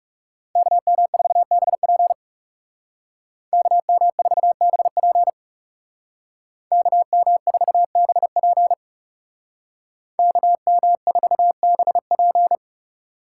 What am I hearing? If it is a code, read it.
Morse code, heard as a keyed tone, about 40 wpm, 700 Hz. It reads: KM4BP KM4BP KM4BP KM4BP